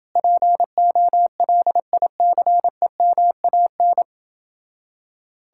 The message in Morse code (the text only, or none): POLICEMAN